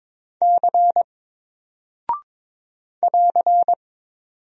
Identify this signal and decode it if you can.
Morse code, heard as a keyed tone, about 22 wpm, 700 Hz. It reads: C <AR>
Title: Single Character (Letter:Number:Punctuation:Prosign) Alt 22wpm